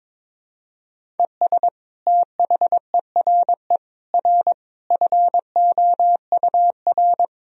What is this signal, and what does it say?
Morse code, heard as a keyed tone, about 22 wpm, 700 Hz. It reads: ES THERE R FOUR
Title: Sentences from Top 300 Words and Abbreviations 22wpm